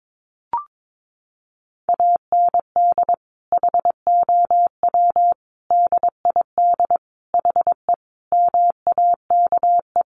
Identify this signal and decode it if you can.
Morse code, heard as a keyed tone, about 22 wpm, 700 Hz. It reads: AND HOW DID HE MAKE
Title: Sentences from Top 400 Words 22wpm